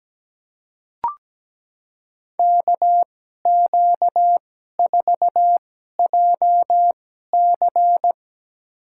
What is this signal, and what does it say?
Morse code, heard as a keyed tone, about 17 wpm, spaced as for 12 wpm, 700 Hz. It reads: KQ4JC
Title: USA Call Signs - 2x Character Spacing - 17wpm